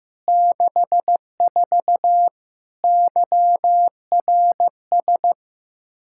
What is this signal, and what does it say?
Morse code, heard as a keyed tone, about 15 wpm, 700 Hz. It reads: 64 YRS
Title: K1USN SST QSO Practice 15wpm